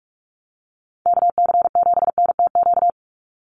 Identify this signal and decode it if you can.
Morse code, heard as a keyed tone, about 44 wpm, 700 Hz. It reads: KC7NTQ
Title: USA Call Signs 45wpm